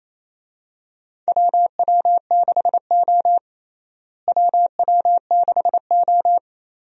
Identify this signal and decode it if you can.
Morse code, heard as a keyed tone, about 28 wpm, 700 Hz. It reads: WW6O WW6O
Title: USA Call Signs - 4 Characters - Repeated 3x - 28wpm